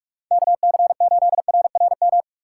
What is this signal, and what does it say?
Morse code, heard as a keyed tone, about 42 wpm, 700 Hz. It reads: KC8RRM
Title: USA Call Signs - 6 Characters - 2x Character Spacing - 45wpm